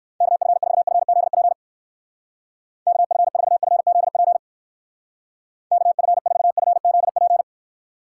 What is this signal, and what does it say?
Morse code, heard as a keyed tone, about 53 wpm, 700 Hz. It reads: KF4FZP KF4FZP KF4FZP
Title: Speed Racing - USA 2x3 Call Signs 45wpm